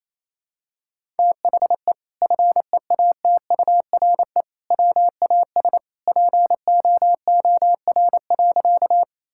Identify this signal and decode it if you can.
Morse code, heard as a keyed tone, about 28 wpm, 700 Hz. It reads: THE FEATURE WAS POOR.
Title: Sentences from Top 1000 Words - Review Part 5 - 28wpm